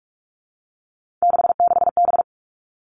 Morse code, heard as a keyed tone, about 46 wpm, 700 Hz. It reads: 66B
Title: Three Challenging Letter-Number Combinations 45wpm